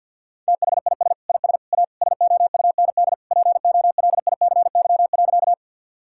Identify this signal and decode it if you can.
Morse code, heard as a keyed tone, about 50 wpm, 700 Hz. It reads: THIS IS A SOUND POLICY.
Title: Sentences from Top 1000 Words - Review Part 5 - 50wpm